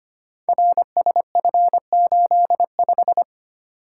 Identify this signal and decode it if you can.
Morse code, heard as a keyed tone, about 25 wpm, 700 Hz. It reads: RSF85
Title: Code Groups - 5 Letter:Numbers - 25wpm